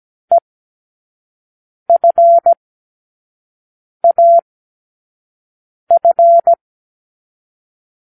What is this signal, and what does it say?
Morse code, heard as a keyed tone, about 17 wpm, 680 Hz. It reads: E F A F
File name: k7qo / Lesson9(A-F)